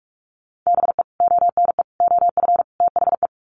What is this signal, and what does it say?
Morse code, heard as a keyed tone, about 44 wpm, 700 Hz. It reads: BE ONE OF THE
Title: Sets of 4 Words 45wpm